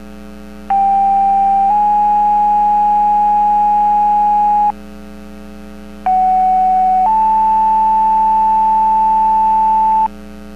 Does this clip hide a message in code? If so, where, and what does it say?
Yes: two-tone page from 0.7 to 10.1 s, heard as a one-second tone, then a three-second tone, about 790 then 830 Hz, about 750 then 880 Hz.